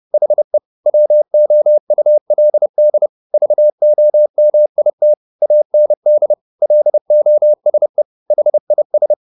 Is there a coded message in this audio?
Yes: Morse code from 0.1 to 9.1 s, heard as a keyed tone, about 30 wpm, 600 Hz. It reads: HE WOULD VOMIT AND LOSE HIS